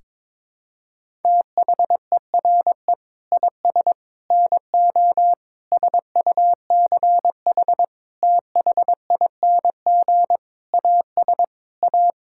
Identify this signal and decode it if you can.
Morse code, heard as a keyed tone, about 22 wpm, 700 Hz. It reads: THERE IS NO SUCH THING AS A
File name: Sentences with 7-Word Phrases - 22wpm